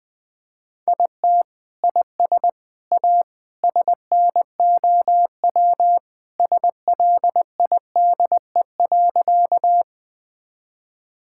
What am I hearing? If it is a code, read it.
Morse code, heard as a keyed tone, about 20 wpm, 700 Hz. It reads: IT IS A SNOW SLIDE.
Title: Sentences from Top 1600 Words 20wpm